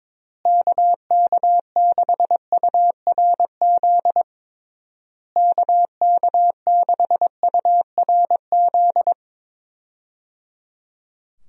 Morse code, heard as a keyed tone, about 22 wpm, 700 Hz. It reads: KK6URZ KK6URZ
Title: USA Call Signs - 6 Characters - Repeated 3x - 22wpm